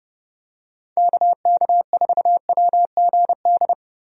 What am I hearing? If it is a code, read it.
Morse code, heard as a keyed tone, about 32 wpm, 700 Hz. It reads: KK4WGD
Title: USA Call Signs - 6 Characters - Repeated 3x - 30wpm